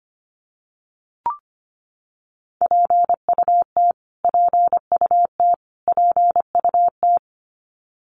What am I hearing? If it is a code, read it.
Morse code, heard as a keyed tone, about 25 wpm, 700 Hz. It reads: PUT PUT PUT
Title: Top 500 Words Repeated 25wpm